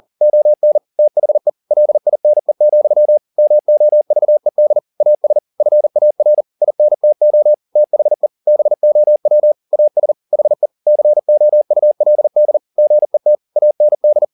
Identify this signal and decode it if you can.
Morse code, heard as a keyed tone, about 40 wpm, 600 Hz. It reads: ON THE LINE, MOVED AS FAR INTO THE BOW AS HE COULD GET AND